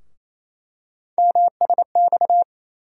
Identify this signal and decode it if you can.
Morse code, heard as a keyed tone, about 28 wpm, 700 Hz. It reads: MSX
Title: USA Call Sign Postfixes - 3 Letters 28wpm